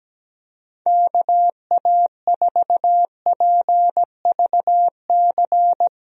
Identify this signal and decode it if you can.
Morse code, heard as a keyed tone, about 17 wpm, 700 Hz. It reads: KA4PVC
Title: USA Call Signs - 6 Characters - 17wpm